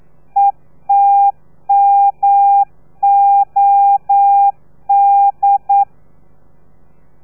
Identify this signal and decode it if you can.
Morse code, heard as a keyed tone, about 9 wpm, 800 Hz. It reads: ETMOD
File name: telegrafia / i2tfj / Lez3 40